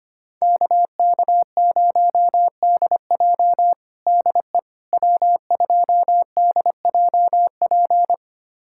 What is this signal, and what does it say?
Morse code, heard as a keyed tone, about 25 wpm, 700 Hz. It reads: KK0DJ DE W2DJP
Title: USA Callsign DE Callsign - 5 Characters - 25wpm